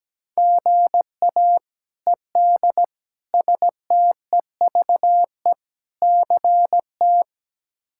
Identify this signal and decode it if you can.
Morse code, heard as a keyed tone, about 17 wpm, 700 Hz. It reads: GA ED STEVE CT
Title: K1USN SST QSO Practice - 17wpm